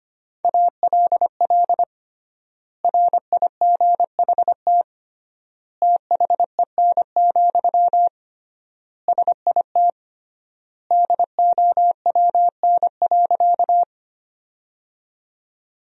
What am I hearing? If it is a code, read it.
Morse code, heard as a keyed tone, about 25 wpm, 700 Hz. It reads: ALL RIGHT THEN, SIT DOWN.
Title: 3x Word-Spacing - Sentences from Top 500 Words - 25wpm